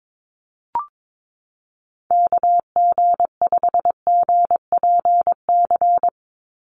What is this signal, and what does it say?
Morse code, heard as a keyed tone, about 22 wpm, 700 Hz. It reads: KG5GPC
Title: USA Call Signs 22wpm